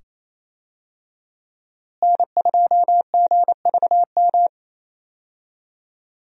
Morse code, heard as a keyed tone, about 28 wpm, 700 Hz. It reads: N2GVM
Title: USA Call Signs - 5 Characters - Repeated 2x - 28wpm